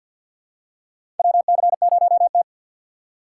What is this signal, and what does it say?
Morse code, heard as a keyed tone, about 48 wpm, 700 Hz. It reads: WC0T
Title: USA Call Signs - 4 Characters - 2x Character Spacing - 50wpm